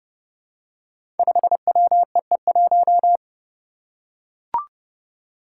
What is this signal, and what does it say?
Morse code, heard as a keyed tone, about 28 wpm, 700 Hz. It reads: 5WEE1
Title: Five Challenging Letter-Number Combinations 30wpm